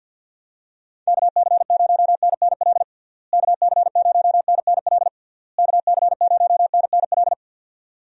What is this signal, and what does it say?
Morse code, heard as a keyed tone, about 51 wpm, 700 Hz. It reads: KC0NNL KC0NNL KC0NNL
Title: USA Call Signs - 6 Characters - Repeated 3x - 50wpm